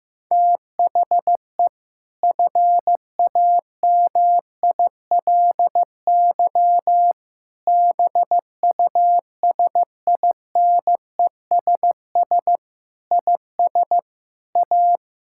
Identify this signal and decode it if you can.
Morse code, heard as a keyed tone, about 15 wpm, 700 Hz. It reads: THE FAMILY BUSINESS IS A
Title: Sentences from Top 1000 Words - Review Part 6 - 15wpm